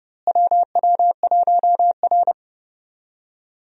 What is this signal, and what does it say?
Morse code, heard as a keyed tone, about 30 wpm, 700 Hz. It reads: WW1R
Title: USA Call Signs - 4 Characters - Repeated 2x - 30wpm